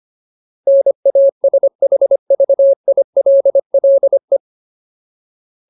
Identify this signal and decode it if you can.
Morse code, heard as a keyed tone, about 25 wpm, 550 Hz. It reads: NASHVILLE
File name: Ultimate Word List / 25WPM Ultimate